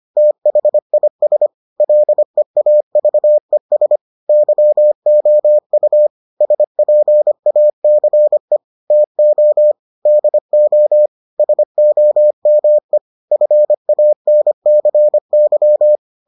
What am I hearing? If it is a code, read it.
Morse code, heard as a keyed tone, about 25 wpm, 600 Hz. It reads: THIS LEAVES YOU SPACE TO DO SOME FANCY